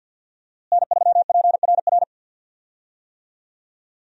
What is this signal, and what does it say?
Morse code, heard as a keyed tone, about 52 wpm, 700 Hz. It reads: N3PRR
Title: USA Call Signs - 5 Characters - Repeated 3x - 50wpm